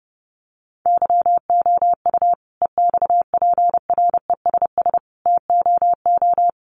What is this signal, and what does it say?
Morse code, heard as a keyed tone, about 30 wpm, 700 Hz. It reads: YOU EXPRESS TOO
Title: Sentences from Top 1100 Words 30wpm